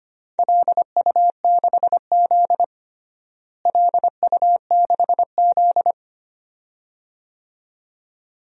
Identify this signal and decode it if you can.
Morse code, heard as a keyed tone, about 25 wpm, 700 Hz. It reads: LU6Z LU6Z
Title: DXCC-100 Call Signs - 4 Characters - Repeated 2x - 25wpm